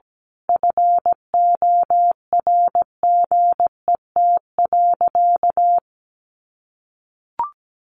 Morse code, heard as a keyed tone, about 17 wpm, 700 Hz. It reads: FORGET.
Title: Sentences from Top 1500 Words 17wpm